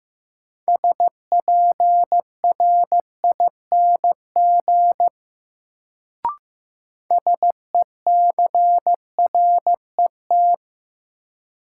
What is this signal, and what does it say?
Morse code, heard as a keyed tone, about 15 wpm, 700 Hz. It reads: SPRING SECRET
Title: Sentences from Top 1200 Words - New Word Drill Alt - 15wpm